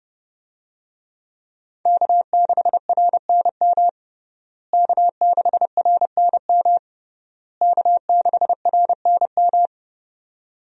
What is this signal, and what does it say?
Morse code, heard as a keyed tone, about 30 wpm, 700 Hz. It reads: K6RNM K6RNM K6RNM
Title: USA Call Signs - 5 Characters - Repeated 3x - 30wpm